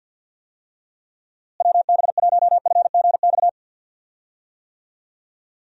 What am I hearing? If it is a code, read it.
Morse code, heard as a keyed tone, about 46 wpm, 700 Hz. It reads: WB1FGX
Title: USA Call Signs - 6 Characters - Repeated 3x - 50wpm